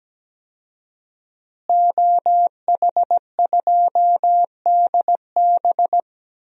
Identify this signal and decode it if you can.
Morse code, heard as a keyed tone, about 17 wpm, 700 Hz. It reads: OH2DB
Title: DXCC-100 Call Signs - 5 Characters - Repeated 2x - 17wpm